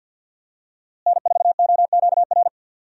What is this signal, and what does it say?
Morse code, heard as a keyed tone, about 51 wpm, 700 Hz. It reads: N4OQR
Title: USA Call Signs - 5 Characters - Repeated 2x - 50wpm